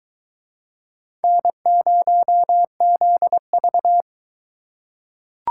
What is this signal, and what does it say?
Morse code, heard as a keyed tone, about 22 wpm, 700 Hz. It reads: N0ZV